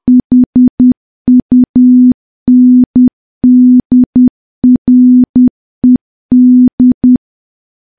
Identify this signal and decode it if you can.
Morse code, heard as a keyed tone, about 10 wpm, 250 Hz. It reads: HUNDRED
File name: morse / morse-training-10WPM